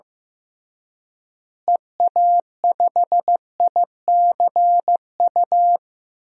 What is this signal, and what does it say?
Morse code, heard as a keyed tone, about 15 wpm, 700 Hz. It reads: EA5ICU
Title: DXCC-100 Call Signs - 6 Characters - Repeated 3x - 15wpm